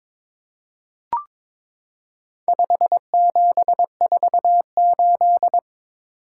Morse code, heard as a keyed tone, about 22 wpm, 700 Hz. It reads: 5748
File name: practice / 22wpm / Four Letter-Number Combinations - 22wpm